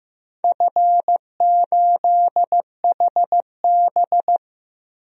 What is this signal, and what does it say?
Morse code, heard as a keyed tone, about 15 wpm, 700 Hz. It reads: F8HB
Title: DXCC-100 Call Signs - 4 Characters - 15wpm